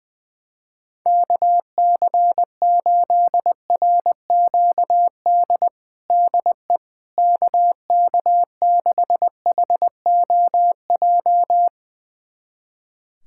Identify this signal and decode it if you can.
Morse code, heard as a keyed tone, about 20 wpm, 700 Hz. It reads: KC8RQD DE KK6HOJ
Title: USA Callsign DE Callsign - 6 Characters - 20wpm